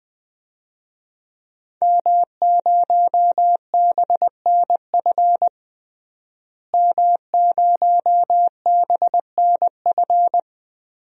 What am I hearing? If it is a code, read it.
Morse code, heard as a keyed tone, about 20 wpm, 700 Hz. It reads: M0BNF M0BNF
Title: DXCC-100 Call Signs - 5 Characters - Repeated 2x - 20wpm